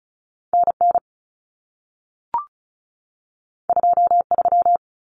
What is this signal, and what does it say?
Morse code, heard as a keyed tone, about 36 wpm, 700 Hz. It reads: NN 23
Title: Common Cut Numbers - 2-Digit 35wpm